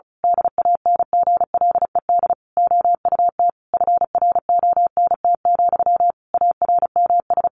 Morse code, heard as a keyed tone, about 34 wpm, 700 Hz. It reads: DANGLED OUT FRONT, ARMS
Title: A Sound Of Thunder 35wpm